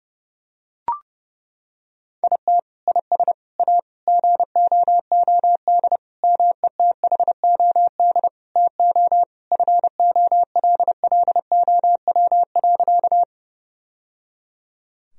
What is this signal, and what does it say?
Morse code, heard as a keyed tone, about 30 wpm, 700 Hz. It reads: IT IS A GOOD METHOD TO FOLLOW.
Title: Sentences from Top 1000 Words - Review Part 3 - 30wpm